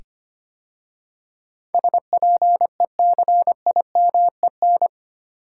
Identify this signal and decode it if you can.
Morse code, heard as a keyed tone, about 25 wpm, 700 Hz. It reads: SPECIMEN
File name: 8 Letter Words - First 3 Letters Given - 25wpm